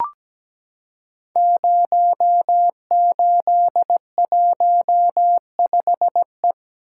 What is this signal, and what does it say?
Morse code, heard as a keyed tone, about 17 wpm, 700 Hz. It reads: 0815E
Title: Code Groups - 5 Letter:Numbers - 17wpm